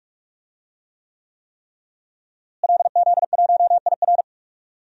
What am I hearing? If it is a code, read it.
Morse code, heard as a keyed tone, about 46 wpm, 700 Hz. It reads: LZ1IR
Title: DXCC-100 Call Signs - 5 Characters - Repeated 3x - 45wpm